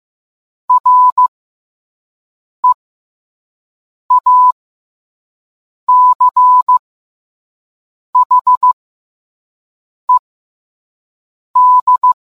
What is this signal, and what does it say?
Morse code, heard as a keyed tone, about 15 wpm, spaced as for 5 wpm, 1000 Hz. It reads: REACHED